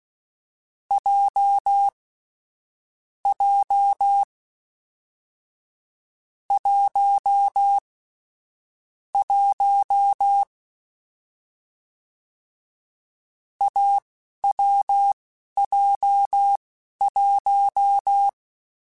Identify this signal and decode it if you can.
Morse code, heard as a keyed tone, about 16 wpm, spaced as for 11 wpm, 800 Hz. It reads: J J 1 1 AWJ1
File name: Short Runs / Short Runs-mp3s / awj1ndb6gz7 @10